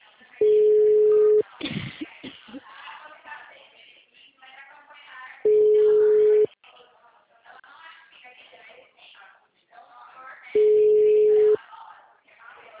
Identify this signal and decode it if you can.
Ringback tone (Europe, 425 Hz), with speech.